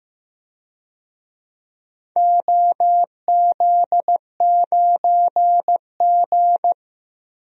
Morse code, heard as a keyed tone, about 15 wpm, 700 Hz. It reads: OZ9G
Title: DXCC-100 Call Signs - 4 Characters - 15wpm